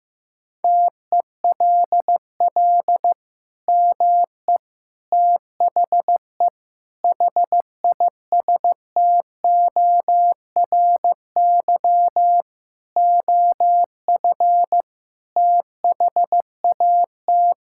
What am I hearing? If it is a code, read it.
Morse code, heard as a keyed tone, about 15 wpm, 700 Hz. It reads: TELL ME THE HISTORY OF THAT
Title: Sentences from Top 1000 Words - Review Part 5 - 15wpm